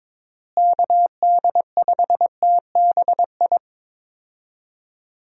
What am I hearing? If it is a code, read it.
Morse code, heard as a keyed tone, about 21 wpm, 700 Hz. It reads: KD5TBI
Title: USA Call Signs - 6 Characters - Repeated 2x - 22wpm